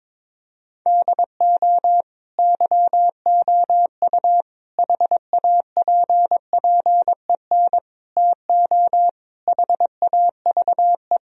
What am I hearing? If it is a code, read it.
Morse code, heard as a keyed tone, about 22 wpm, 700 Hz. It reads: DO YOU HAPPEN TO HAVE
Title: Sentences from Top 1000 Words - Review Part 2 - 22wpm